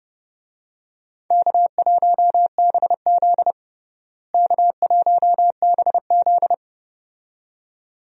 Morse code, heard as a keyed tone, about 32 wpm, 700 Hz. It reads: K1BZ K1BZ
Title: USA Call Signs - 4 Characters - Repeated 2x - 30wpm